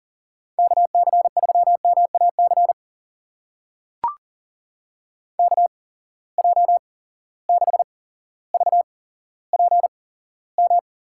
Morse code, heard as a keyed tone, about 40 wpm, spaced as for 17 wpm, 700 Hz. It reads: KC3MAC KJ6VPM
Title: USA Call Signs - 6 Characters - 8x Character Spacing - 40wpm